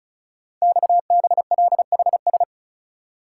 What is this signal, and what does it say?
Morse code, heard as a keyed tone, about 36 wpm, 690 Hz. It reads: XBLHS
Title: Code Groups - 5 Letters - 35wpm